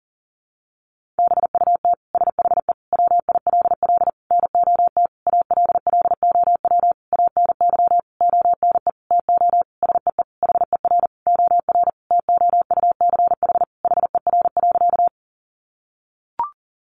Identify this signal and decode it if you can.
Morse code, heard as a keyed tone, about 37 wpm, 700 Hz. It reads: BUT SHE WILL NOT ALLOW ANY ONE TO SEE HER OR TOUCH HER.
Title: Sentences from Top 1200 Words 40wpm